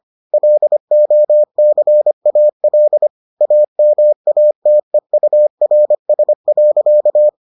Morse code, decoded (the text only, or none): LOCAL AMATEURS.